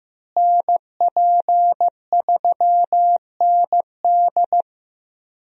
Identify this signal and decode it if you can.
Morse code, heard as a keyed tone, about 16 wpm, 700 Hz. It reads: NP3ND
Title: USA Call Signs - 5 Characters - Repeated 3x - 15wpm